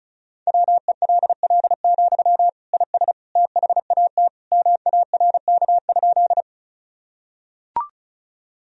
Morse code, heard as a keyed tone, about 36 wpm, 700 Hz. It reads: WELL, IS THAT MARK?
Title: Sentences from Top 1000 Words - Review Part 1 - 35wpm